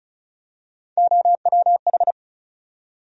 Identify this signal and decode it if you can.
Morse code, heard as a keyed tone, about 36 wpm, 700 Hz. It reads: OWH
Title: USA Call Sign Postfixes - 3 Letters 35wpm